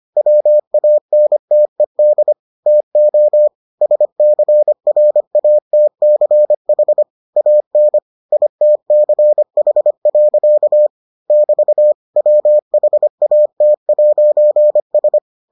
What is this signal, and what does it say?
Morse code, heard as a keyed tone, about 25 wpm, 600 Hz. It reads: WANTED TO SCRATCH AN ITCH. <BT> WHAT'S